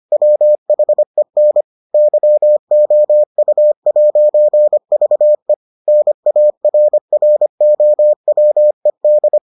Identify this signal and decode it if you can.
Morse code, heard as a keyed tone, about 25 wpm, 600 Hz. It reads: WHEN YOU'VE NARROWED